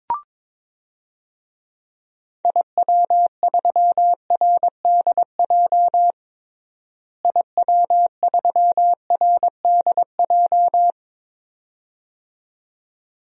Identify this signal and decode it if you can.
Morse code, heard as a keyed tone, about 21 wpm, 700 Hz. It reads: IW3RDJ IW3RDJ
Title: DXCC-100 Call Signs - 6 Characters - Repeated 2x - 22wpm